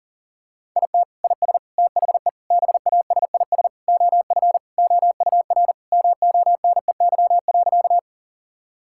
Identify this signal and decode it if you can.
Morse code, heard as a keyed tone, about 40 wpm, 700 Hz. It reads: IT IS THE BASIS OF OUR MONEY.
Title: Sentences from Top 1000 Words - Review Part 5 - 40wpm